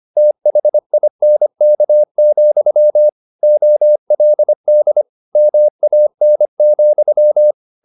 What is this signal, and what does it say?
Morse code, heard as a keyed tone, about 25 wpm, 600 Hz. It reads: THINK, OLD MAN,
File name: the old man and the sea part 1 25wpm